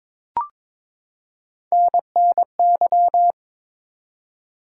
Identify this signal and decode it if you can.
Morse code, heard as a keyed tone, about 22 wpm, 700 Hz. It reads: NNY
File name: practice / 22wpm / ARRL Sections v2 - 22wpm